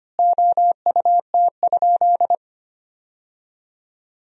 Morse code, heard as a keyed tone, about 25 wpm, 700 Hz. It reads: OUT?
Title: 3x Word-Spacing - Sentences from Top 300 Words - 25wpm